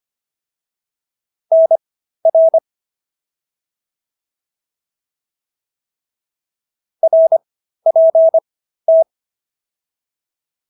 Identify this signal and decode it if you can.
Morse code, heard as a keyed tone, about 25 wpm, spaced as for 13 wpm, 650 Hz. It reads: NR RPT